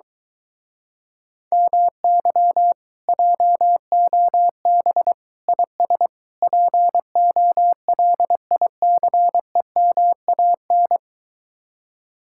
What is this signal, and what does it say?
Morse code, heard as a keyed tone, about 23 wpm, 700 Hz. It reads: MY JOB IS POLICEMAN